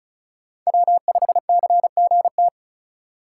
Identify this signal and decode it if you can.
Morse code, heard as a keyed tone, about 34 wpm, 700 Hz. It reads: W5CGT
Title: USA Call Signs - 5 Characters - 3x Character Spacing - 35wpm